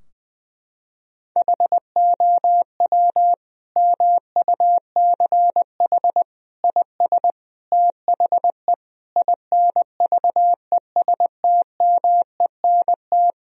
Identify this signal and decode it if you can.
Morse code, heard as a keyed tone, about 20 wpm, 700 Hz. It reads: HOW MUCH IS THE INVESTMENT
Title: Sentences from Top 1000 Words - Review Part 4 - 20wpm